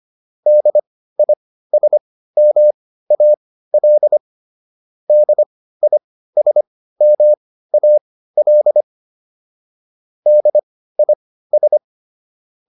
Morse code, heard as a keyed tone, about 25 wpm, spaced as for 15 wpm, 600 Hz. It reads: DISMAL DISMAL DIS
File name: DIS6 15